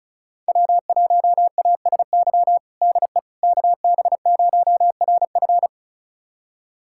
Morse code, heard as a keyed tone, about 34 wpm, 700 Hz. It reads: W1ASY DE KB0RF
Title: USA Callsign DE Callsign - 5 Characters - 35wpm